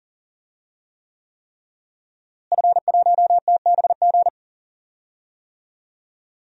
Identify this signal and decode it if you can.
Morse code, heard as a keyed tone, about 40 wpm, 700 Hz. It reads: F1TBG
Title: DXCC-100 Call Signs - 5 Characters - Repeated 3x - 40wpm